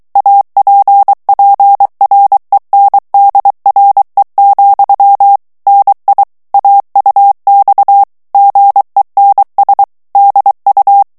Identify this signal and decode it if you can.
Morse code, heard as a keyed tone, about 23 wpm, 800 Hz. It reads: APPRENDRE, NI AUX GENS DU
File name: FRANCAIS TEXTE CLAIR 2 VITESSE 22 MOTS MINUTES